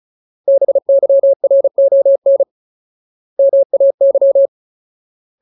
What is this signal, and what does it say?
Morse code, heard as a keyed tone, about 34 wpm, 550 Hz. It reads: BYRON MAY
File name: Ultimate Word List / 35WPM Ultimate